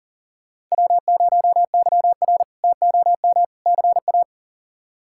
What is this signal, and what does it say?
Morse code, heard as a keyed tone, about 40 wpm, 700 Hz. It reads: W0YR TOM CA
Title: K1USN SST QSO Practice - 40wpm